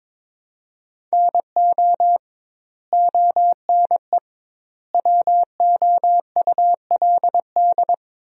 Morse code, heard as a keyed tone, about 22 wpm, 700 Hz. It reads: NO ONE WOULD
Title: 2x Word-Spacing - Sentences from Top 300 Words - 22wpm